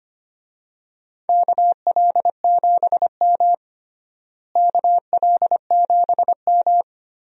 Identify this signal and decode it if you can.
Morse code, heard as a keyed tone, about 25 wpm, 700 Hz. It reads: KL7M KL7M
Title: DXCC-100 Call Signs - 4 Characters - Repeated 3x - 25wpm